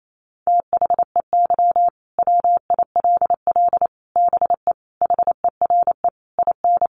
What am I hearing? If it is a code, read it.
Morse code, heard as a keyed tone, about 28 wpm, 700 Hz. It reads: THEY WILL BE HERE IN
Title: Sentences from Top 1000 Words 28wpm